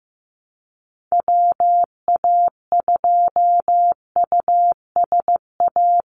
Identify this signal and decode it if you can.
Morse code, heard as a keyed tone, about 15 wpm, 700 Hz. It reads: WA2USA
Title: NA SOTA Activator Callsigns 15wpm